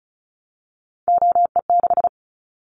Morse code, heard as a keyed tone, about 36 wpm, 700 Hz. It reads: OE6
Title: Callsign Prefixes for Top 100 DXCC - 35wpm